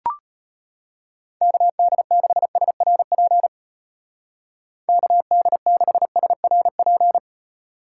Morse code, heard as a keyed tone, about 33 wpm, 700 Hz. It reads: KD6SRP KD6SRP